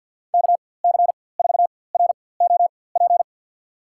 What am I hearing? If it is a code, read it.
Morse code, heard as a keyed tone, about 52 wpm, spaced as for 23 wpm, 700 Hz. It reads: KC4ROP
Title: USA Call Signs - 4x Character Spacing - 50wpm